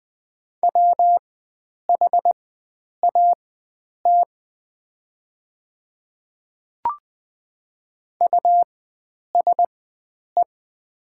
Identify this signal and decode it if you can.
Morse code, heard as a keyed tone, about 20 wpm, spaced as for 9 wpm, 700 Hz. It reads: WHAT USE
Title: 4x Character Spacing - Top 100 Words - 20wpm